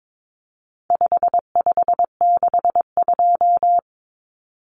Morse code, heard as a keyed tone, about 22 wpm, 700 Hz. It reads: 5562
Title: Common Cut Numbers - 4-Digit 22wpm